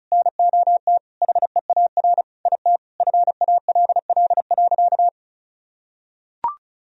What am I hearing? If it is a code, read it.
Morse code, heard as a keyed tone, about 36 wpm, 700 Hz. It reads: NOT HEAR IT FALL.
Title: Sentences from Top 1000 Words - Review Part 2 - 35wpm